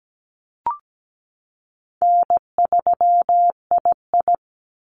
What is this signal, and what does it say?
Morse code, heard as a keyed tone, about 17 wpm, 700 Hz. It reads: N3II
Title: NA SOTA Activator Callsigns 17wpm